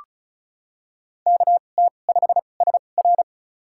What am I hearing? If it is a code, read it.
Morse code, heard as a keyed tone, about 36 wpm, spaced as for 25 wpm, 700 Hz. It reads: KT5SR
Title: USA Call Signs - 5 Characters - 2x Character Spacing - 35wpm